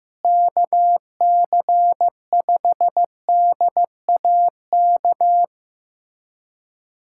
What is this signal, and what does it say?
Morse code, heard as a keyed tone, about 15 wpm, 700 Hz. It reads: KC5DAK
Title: USA Callsign DE Callsign 15wpm